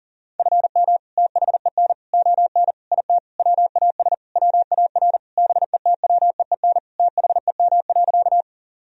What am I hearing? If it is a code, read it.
Morse code, heard as a keyed tone, about 37 wpm, 700 Hz. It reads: FM THEN ON IT WAS WAR BETWEEN THEM.
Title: Sentences from Top 400 Words and Abbreviations 40wpm